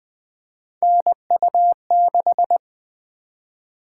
Morse code, heard as a keyed tone, about 20 wpm, 700 Hz. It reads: NU6
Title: Three Letter-Number Combinations - 20wpm